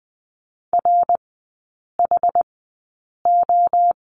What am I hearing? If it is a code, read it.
Morse code, heard as a keyed tone, about 20 wpm, spaced as for 8 wpm, 700 Hz. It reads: RHO
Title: 060 Lesson 11e - Letter h - Three Characters 20wpm